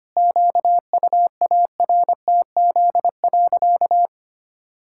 Morse code, heard as a keyed tone, about 25 wpm, 700 Hz. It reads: QUARTZ.
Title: Pangram Challenge v2 - 25wpm